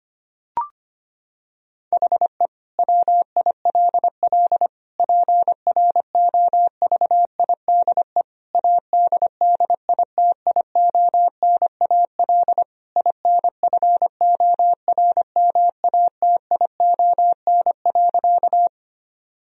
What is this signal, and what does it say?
Morse code, heard as a keyed tone, about 25 wpm, 700 Hz. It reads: HE WILL PROVIDE ADDITIONAL INFORMATION.
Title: Sentences from Top 1000 Words - Review Part 4 - 25wpm